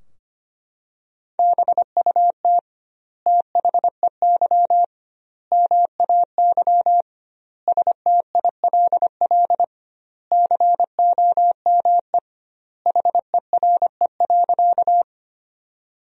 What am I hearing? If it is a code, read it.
Morse code, heard as a keyed tone, about 25 wpm, 700 Hz. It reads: BUT THEY MAY STILL COME HERE.
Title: 2x Word-Spacing - Sentences from Top 300 Words - 25wpm